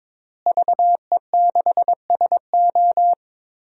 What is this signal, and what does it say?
Morse code, heard as a keyed tone, about 22 wpm, 700 Hz. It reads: VE6SO
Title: DXCC-100 Call Signs - 5 Characters - Repeated 3x - 22wpm